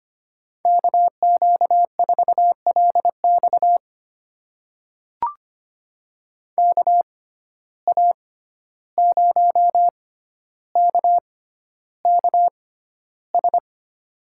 Morse code, heard as a keyed tone, about 25 wpm, spaced as for 12 wpm, 700 Hz. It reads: KQ4LX KA0KKS
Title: USA Call Signs - 6x Character Spacing - 25wpm